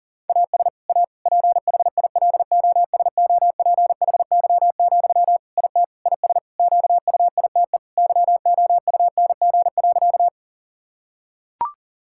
Morse code, heard as a keyed tone, about 37 wpm, 700 Hz. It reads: AS A PHILOSOPHY, IT IS QUITE YOUNG.
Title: Sentences from Top 1000 Words - Review Part 6 - 40wpm